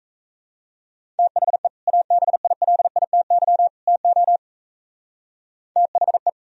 Morse code, heard as a keyed tone, about 34 wpm, 700 Hz. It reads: THE ABILITY TO THE
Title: Speed-Racing - Sets of 3 Words 28wpm